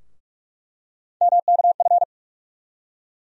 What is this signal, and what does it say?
Morse code, heard as a keyed tone, about 46 wpm, 700 Hz. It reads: MKF
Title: USA Call Sign Postfixes - 3 Letters 45wpm